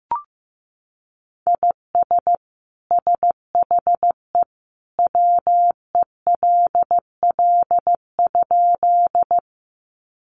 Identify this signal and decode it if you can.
Morse code, heard as a keyed tone, about 14 wpm, 700 Hz. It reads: IS SHE WELL?
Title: Sentences from Top 200 Words 15wpm